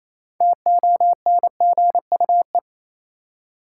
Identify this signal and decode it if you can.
Morse code, heard as a keyed tone, about 28 wpm, 700 Hz. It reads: TONGUE
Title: Six Letter Words - 28wpm